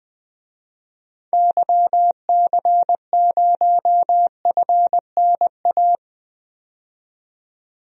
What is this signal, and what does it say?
Morse code, heard as a keyed tone, about 20 wpm, 700 Hz. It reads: YC0FNA
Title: DXCC-100 Call Signs - 6 Characters - 20wpm